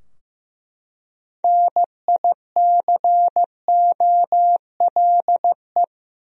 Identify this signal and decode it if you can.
Morse code, heard as a keyed tone, about 15 wpm, 700 Hz. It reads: NICOLE
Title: USA Common Names - 15wpm